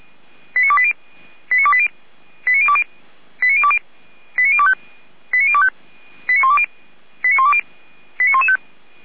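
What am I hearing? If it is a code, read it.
Five-tone selective call (ZVEI), heard as quick runs of five tones, tones of about 70 ms. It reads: 89280 89280 89020 89020 89025 89025 89120 89120 89115
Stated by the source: ZVEI five-tone selective call received on SDR, from the Feuerwehr Orlamünde audio server.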